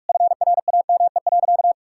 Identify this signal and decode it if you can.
Morse code, heard as a keyed tone, about 46 wpm, 700 Hz. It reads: FRAME.
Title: Sentences from Top 1000 Words - Review Part 6 - 45wpm